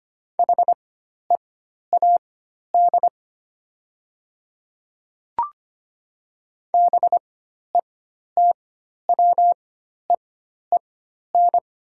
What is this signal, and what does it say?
Morse code, heard as a keyed tone, about 25 wpm, spaced as for 12 wpm, 690 Hz. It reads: HEAD BETWEEN
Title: 4x Character Spacing - Top 300 Words - 25wpm